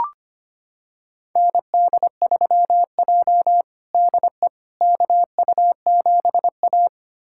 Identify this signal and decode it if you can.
Morse code, heard as a keyed tone, about 25 wpm, 700 Hz. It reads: ND3J DE KU7A
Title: USA Callsign DE Callsign - 4 Characters - 25wpm